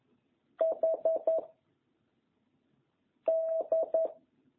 Morse code, heard as a keyed tone, about 11 wpm, 660 Hz. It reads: H D